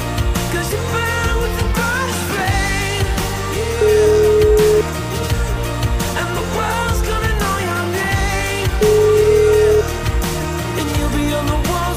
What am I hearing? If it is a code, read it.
Ringback tone (Europe, 425 Hz), with noise.